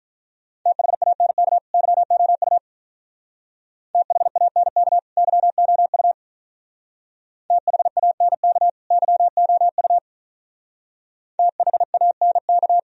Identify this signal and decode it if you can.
Morse code, heard as a keyed tone, about 43 wpm, 700 Hz. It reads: THANK YOU THANK YOU THANK YOU THANK
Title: Speed-Racing - Sets of 2 Words 35wpm